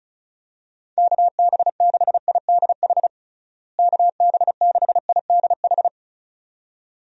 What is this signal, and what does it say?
Morse code, heard as a keyed tone, about 34 wpm, 700 Hz. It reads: KB6IDH KB6IDH
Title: USA Call Signs - 6 Characters - Repeated 2x - 35wpm